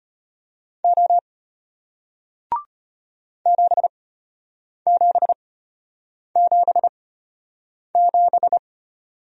Morse code, heard as a keyed tone, about 33 wpm, spaced as for 8 wpm, 700 Hz. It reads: O 7777